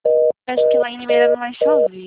Reorder tone (North America), 480 + 620 Hz, with noise.
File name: out- -2012- - - .14887